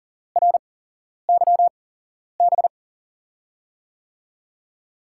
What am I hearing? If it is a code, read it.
Morse code, heard as a keyed tone, about 40 wpm, spaced as for 11 wpm, 700 Hz. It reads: RYB
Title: USA Call Signs - 6 Characters - 8x Character Spacing - 40wpm